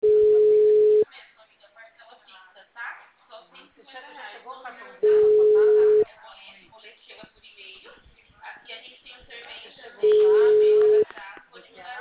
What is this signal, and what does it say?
Ringback tone (Europe, 425 Hz), with speech.